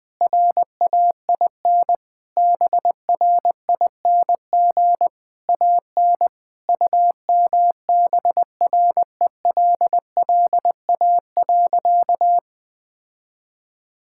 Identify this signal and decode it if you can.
Morse code, heard as a keyed tone, about 21 wpm, 700 Hz. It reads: RAIN BRING AN UMBRELLA.
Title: Sentences with 3-Word Phrases - 20wpm